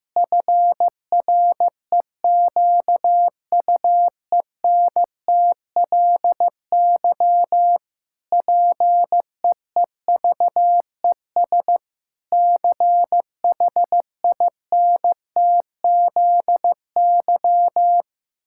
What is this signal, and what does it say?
Morse code, heard as a keyed tone, about 14 wpm, 700 Hz. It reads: FREQUENTLY PEEVES CHINTZY
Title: Pangram Challenge v2 - 15wpm